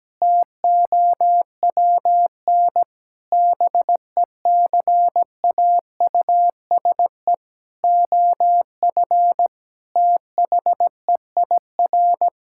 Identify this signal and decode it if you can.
Morse code, heard as a keyed tone, about 17 wpm, 700 Hz. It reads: TOWN BECAUSE OF THEIR
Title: Sentences from Top 1000 Words - Review Part 5 - 17wpm